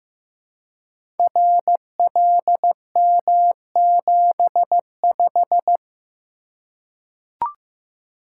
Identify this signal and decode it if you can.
Morse code, heard as a keyed tone, about 15 wpm, 700 Hz. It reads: RLM75
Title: Code Groups - 5 Letter:Numbers - 15wpm